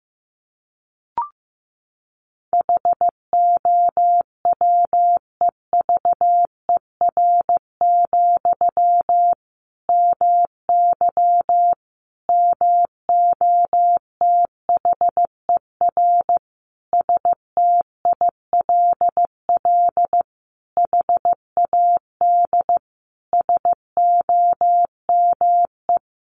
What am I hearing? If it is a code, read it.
Morse code, heard as a keyed tone, about 15 wpm, 700 Hz. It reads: HOWEVER, MY MOTHER STILL HAD SOME
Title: Sentences from Top 1000 Words 15wpm